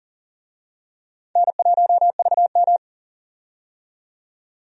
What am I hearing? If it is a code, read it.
Morse code, heard as a keyed tone, about 40 wpm, 700 Hz. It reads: N1VM
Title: USA Call Signs - 4 Characters - 40wpm